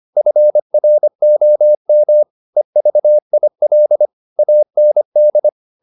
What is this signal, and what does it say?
Morse code, heard as a keyed tone, about 25 wpm, 600 Hz. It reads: FROM EVIL AND